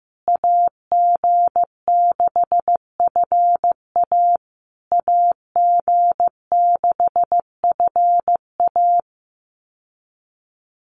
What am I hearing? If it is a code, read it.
Morse code, heard as a keyed tone, about 15 wpm, 700 Hz. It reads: AG6FA AG6FA
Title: USA Call Signs Repeated 3x 15wpm